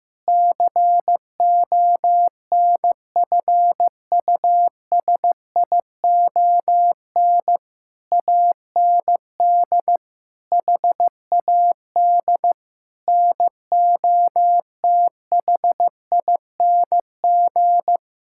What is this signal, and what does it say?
Morse code, heard as a keyed tone, about 15 wpm, 700 Hz. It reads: CONFUSION AND HAD NOTHING
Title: Sentences from Top 1600 Words 15wpm